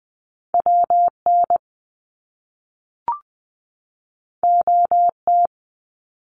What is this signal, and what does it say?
Morse code, heard as a keyed tone, about 20 wpm, 700 Hz. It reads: WN OT
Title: Top Two Letter Combinations in Words 20wpm